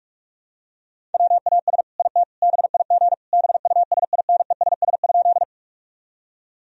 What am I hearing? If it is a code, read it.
Morse code, heard as a keyed tone, about 46 wpm, 700 Hz. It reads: WAS IT BIG BUSINESS?
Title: Sentences from Top 1000 Words - Review Part 1 - 45wpm